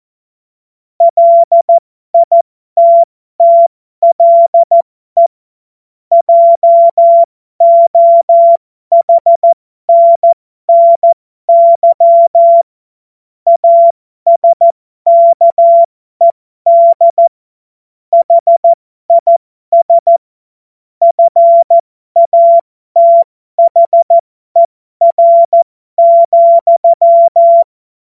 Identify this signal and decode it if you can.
Morse code, heard as a keyed tone, about 14 wpm, spaced as for 12 wpm, 680 Hz. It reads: LITTLE JOHNNY ASKED HIS FATHER,
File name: Morsekurs / Witze